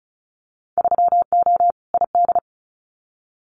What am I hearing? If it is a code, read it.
Morse code, heard as a keyed tone, about 36 wpm, 700 Hz. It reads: 3O ID
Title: Winter Field Day Exchanges Encore 35wpm